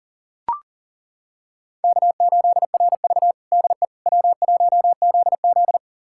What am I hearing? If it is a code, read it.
Morse code, heard as a keyed tone, about 40 wpm, 700 Hz. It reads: K8RV DE W1ZZ
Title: USA Callsign DE Callsign - 4 Characters - 40wpm